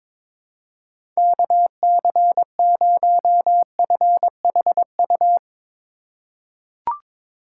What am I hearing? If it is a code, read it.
Morse code, heard as a keyed tone, about 22 wpm, 700 Hz. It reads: KC0FHU
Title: USA Call Signs - 6 Characters - 2x Character Spacing - 22wpm